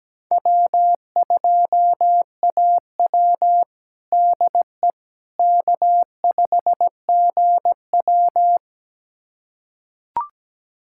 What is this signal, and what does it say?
Morse code, heard as a keyed tone, about 17 wpm, 700 Hz. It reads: W2AW DE K5GW
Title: USA Callsign DE Callsign - 4 Characters - 17wpm